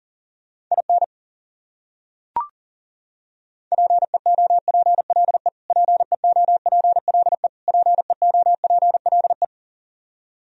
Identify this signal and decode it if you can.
Morse code, heard as a keyed tone, about 40 wpm, 700 Hz. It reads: IN PEOPLE PEOPLE PEOPLE
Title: Top 100 Words Repeated 40wpm